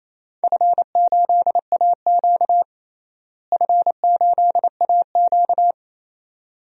Morse code, heard as a keyed tone, about 28 wpm, 700 Hz. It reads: F8AQ F8AQ
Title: DXCC-100 Call Signs - 4 Characters - Repeated 2x - 28wpm